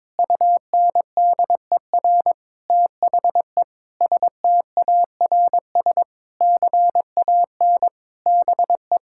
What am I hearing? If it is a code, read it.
Morse code, heard as a keyed tone, about 22 wpm, 700 Hz. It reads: UNDER THE STARS CAN BE
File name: Difficult Sentences with Hints - 22wpm